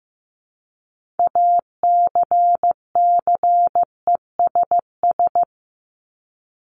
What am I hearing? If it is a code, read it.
Morse code, heard as a keyed tone, about 15 wpm, 700 Hz. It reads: ACCESS
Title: Top 501-750 Words 15wpm